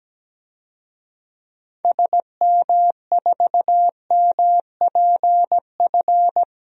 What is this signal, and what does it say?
Morse code, heard as a keyed tone, about 17 wpm, 700 Hz. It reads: SM4MPF
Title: DXCC-100 Call Signs - 6 Characters - Repeated 2x - 17wpm